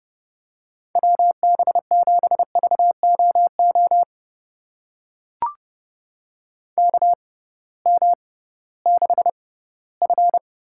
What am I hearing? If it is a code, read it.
Morse code, heard as a keyed tone, about 30 wpm, spaced as for 18 wpm, 700 Hz. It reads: WB7VOO KM6F
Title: USA Call Signs - 6x Character Spacing - 30wpm